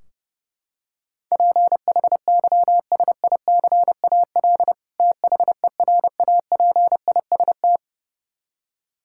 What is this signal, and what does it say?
Morse code, heard as a keyed tone, about 32 wpm, 700 Hz. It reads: PHYSICAL THERAPIST
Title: Practice QSO - Exchange 3 - More Info - 30wpm